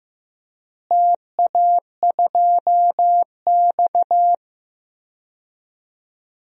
Morse code, heard as a keyed tone, about 15 wpm, 700 Hz. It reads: TA2X
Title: DXCC-100 Call Signs - 4 Characters - Repeated 3x - 15wpm